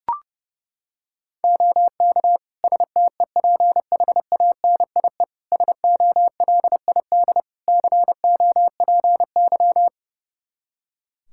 Morse code, heard as a keyed tone, about 30 wpm, 700 Hz. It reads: OK STEPHANIE SOLID COPY
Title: Practice QSO - Exchange 3 - More Info - 30wpm